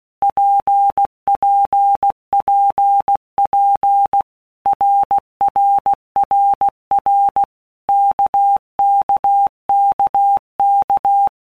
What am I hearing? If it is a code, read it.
Morse code, heard as a keyed tone, about 16 wpm, 800 Hz. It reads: PPPP RRRR KKKK
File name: Basic Letter Groups By Blocks / Groups@15 / RKPX@15